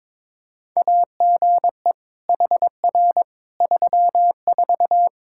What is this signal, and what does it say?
Morse code, heard as a keyed tone, about 21 wpm, 700 Hz. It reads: AGE HR 34
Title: Practice QSO - Exchange 3 - More Info - 22wpm